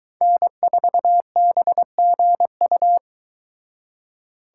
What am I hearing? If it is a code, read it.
Morse code, heard as a keyed tone, about 23 wpm, 700 Hz. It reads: N4BGU